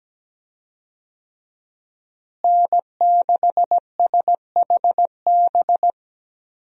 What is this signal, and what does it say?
Morse code, heard as a keyed tone, about 17 wpm, 700 Hz. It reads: N6SHB
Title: USA Call Signs - 5 Characters - Repeated 2x - 17wpm